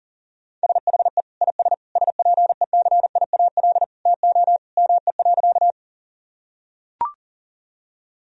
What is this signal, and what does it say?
Morse code, heard as a keyed tone, about 40 wpm, 700 Hz. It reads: SHE IS SPECIAL TO ME.
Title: Sentences from Top 1000 Words - Review Part 3 - 40wpm